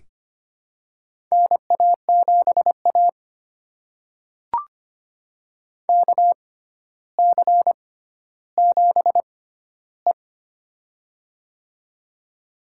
Morse code, heard as a keyed tone, about 25 wpm, spaced as for 13 wpm, 700 Hz. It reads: NA7A KC7E
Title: USA Call Signs - 4 Characters - 6x Character Spacing - 25wpm